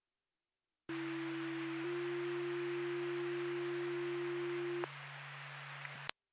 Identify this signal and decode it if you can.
Two-tone page, heard as a one-second tone, then a three-second tone, about 330 then 350 Hz, with noise. It is heard from 0.9 to 4.8 s.